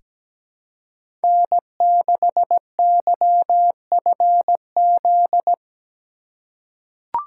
Morse code, heard as a keyed tone, about 17 wpm, 700 Hz. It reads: N6YFZ
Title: USA Call Signs - 5 Characters - 4x Character Spacing - 17wpm